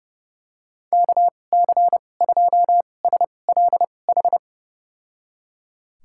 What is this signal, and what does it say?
Morse code, heard as a keyed tone, about 32 wpm, spaced as for 22 wpm, 700 Hz. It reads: KC2SLH
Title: USA Call Signs - 2x Character Spacing - 30wpm